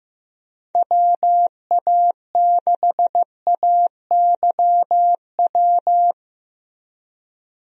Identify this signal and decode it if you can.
Morse code, heard as a keyed tone, about 15 wpm, 700 Hz. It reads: WA6AYW
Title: USA Call Signs - 6 Characters - Repeated 2x - 15wpm